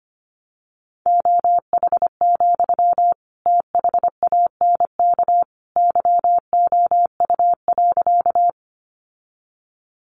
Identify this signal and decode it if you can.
Morse code, heard as a keyed tone, about 25 wpm, 700 Hz. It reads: OH, THANK YOU.
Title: Sentences from Top 600 Words 25wpm